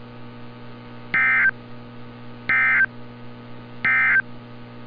Emergency Alert System (SAME) tones, heard as harsh buzzing bursts. It is heard from 1.0 to 4.3 s.
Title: NOAA Weather Radio - Altoona, PA